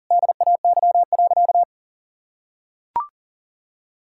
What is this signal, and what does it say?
Morse code, heard as a keyed tone, about 40 wpm, 700 Hz. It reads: DAY.
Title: Sentences from Top 1000 Words - Review Part 5 - 40wpm